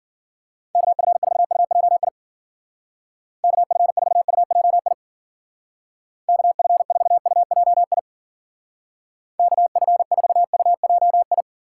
Morse code, heard as a keyed tone, about 52 wpm, 700 Hz. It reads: KF4UJI KF4UJI KF4UJI KF4UJI